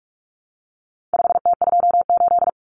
Morse code, heard as a keyed tone, about 44 wpm, 700 Hz. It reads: 5T28
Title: Common Cut Numbers - 4-Digit 45wpm